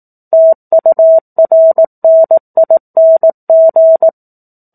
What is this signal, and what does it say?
Morse code, heard as a keyed tone, about 18 wpm, 650 Hz. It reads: TURNING
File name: 04-Chapter-04 18 WPM